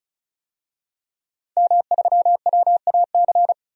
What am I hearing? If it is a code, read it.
Morse code, heard as a keyed tone, about 34 wpm, 700 Hz. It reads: M3WAC
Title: DXCC-100 Call Signs - 5 Characters - 35wpm